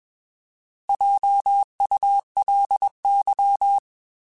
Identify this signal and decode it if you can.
Morse code, heard as a keyed tone, about 21 wpm, 800 Hz. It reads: JULY
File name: Alphabetical / Alphabetical mp3s / Js @20